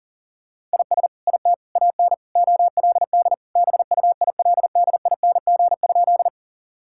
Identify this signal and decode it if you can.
Morse code, heard as a keyed tone, about 40 wpm, 700 Hz. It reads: IS IT AN OLD BUILDING?
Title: Sentences from Top 1000 Words - Review Part 5 - 40wpm